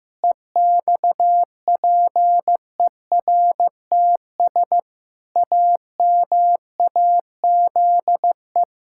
Morse code, heard as a keyed tone, about 15 wpm, 700 Hz. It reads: EXPERTS AMAZE
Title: Pangram Challenge v2 - 15wpm